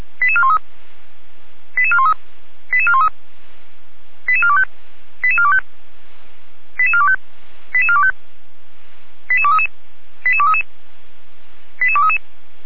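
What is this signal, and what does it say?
Five-tone selective call (ZVEI), heard as quick runs of five tones, tones of about 70 ms. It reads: 88413 88413 88413 88425 88425 88425 88425 88233 88233 88233
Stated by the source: ZVEI five-tone selective call received by SDR.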